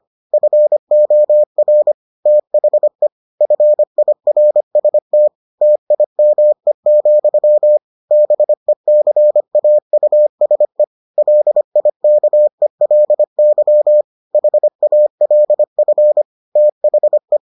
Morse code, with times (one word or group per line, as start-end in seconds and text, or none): 0.3-1.9 FOR
2.3-3.1 THE
3.4-5.3 FIRST
5.6-7.8 TIME,
8.1-10.8 BECAUSE
11.2-14.0 LIKELY
14.3-16.2 HALF
16.6-17.4 THE